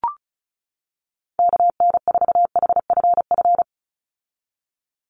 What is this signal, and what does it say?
Morse code, heard as a keyed tone, about 36 wpm, 700 Hz. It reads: KN4HFF
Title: USA Call Signs 35wpm